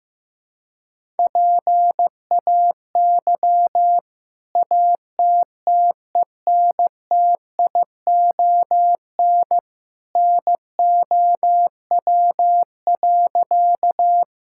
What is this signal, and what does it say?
Morse code, heard as a keyed tone, about 16 wpm, 700 Hz. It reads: PAY ATTENTION NOW.
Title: Sentences from Top 1000 Words - Review Part 6 - 15wpm